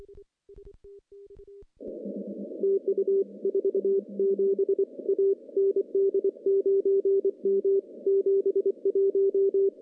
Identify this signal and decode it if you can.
Morse code, heard as a keyed tone, about 26 wpm, 390 Hz. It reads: TU 47 A ND9M 71
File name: contests / CW-SS-k5tr- -0700